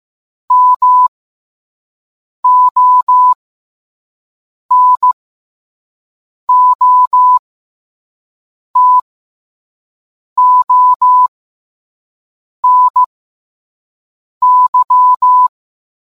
Morse code, heard as a keyed tone, about 15 wpm, spaced as for 5 wpm, 1000 Hz. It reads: MONOTONY